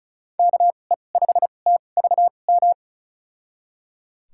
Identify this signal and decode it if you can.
Morse code, heard as a keyed tone, about 36 wpm, spaced as for 25 wpm, 700 Hz. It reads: KE5TVM
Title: USA Call Signs - 6 Characters - 2x Character Spacing - 35wpm